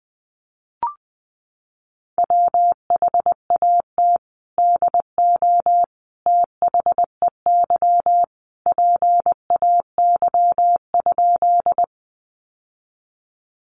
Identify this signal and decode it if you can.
Morse code, heard as a keyed tone, about 19 wpm, 700 Hz. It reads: WHAT DO THEY PAY?
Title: Sentences from Top 700 Words 20wpm